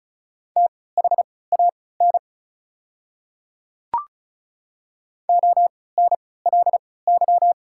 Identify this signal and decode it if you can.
Morse code, heard as a keyed tone, about 36 wpm, spaced as for 20 wpm, 690 Hz. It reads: THAN ONLY
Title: 3x Character Spacing - Top 200 Words - 35wpm